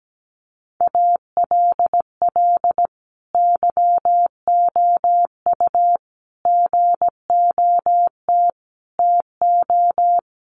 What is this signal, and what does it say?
Morse code, heard as a keyed tone, about 17 wpm, 700 Hz. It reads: ALL YOU GOT TO
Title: A Sound Of Thunder 17wpm